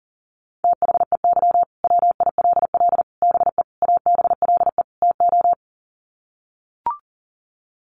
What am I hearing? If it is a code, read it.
Morse code, heard as a keyed tone, about 40 wpm, 700 Hz. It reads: THEY WILL BE ABLE TO
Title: Sets of 5 Words Encore 40wpm